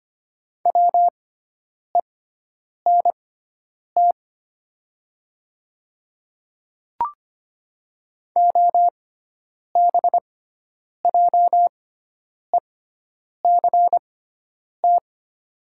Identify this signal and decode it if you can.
Morse code, heard as a keyed tone, about 25 wpm, spaced as for 9 wpm, 700 Hz. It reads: WENT OBJECT
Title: 6x Character Spacing - Top 500 Words - 25wpm